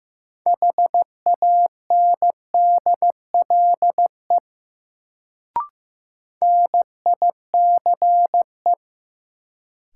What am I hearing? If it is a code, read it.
Morse code, heard as a keyed tone, about 16 wpm, 700 Hz. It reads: HANDLE NICE
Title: Sentences from Top 1100 Words - New Word Drill - 15wpm